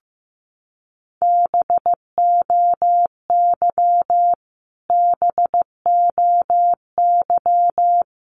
Morse code, heard as a keyed tone, about 15 wpm, 700 Hz. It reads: BOY BOY
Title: Top 300 Words Repeated 15wpm